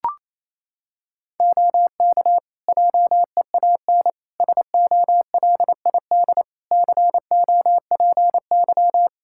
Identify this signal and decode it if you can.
Morse code, heard as a keyed tone, about 28 wpm, 700 Hz. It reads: OK JEAN SOLID COPY
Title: Practice QSO - Exchange 3 - More Info - 28wpm